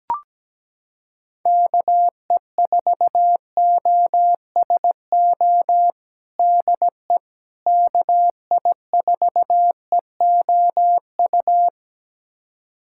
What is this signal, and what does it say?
Morse code, heard as a keyed tone, about 17 wpm, 700 Hz. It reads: KE4OSO DE KI4EOU
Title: USA Callsign DE Callsign - 6 Characters - 17wpm